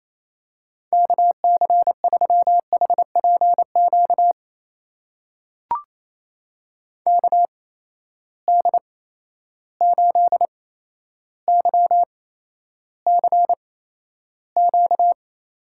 Morse code, heard as a keyed tone, about 27 wpm, spaced as for 12 wpm, 700 Hz. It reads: KC3HPQ KD8YCQ
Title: USA Call Signs - 6 Characters - 8x Character Spacing - 28wpm